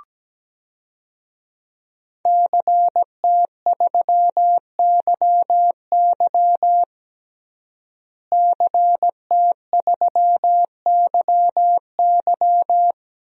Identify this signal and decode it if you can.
Morse code, heard as a keyed tone, about 17 wpm, 700 Hz. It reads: CT3YY CT3YY
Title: DXCC-100 Call Signs - 5 Characters - Repeated 3x - 17wpm